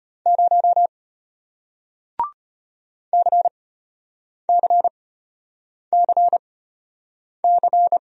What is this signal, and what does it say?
Morse code, heard as a keyed tone, about 33 wpm, spaced as for 8 wpm, 700 Hz. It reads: 0 CCCC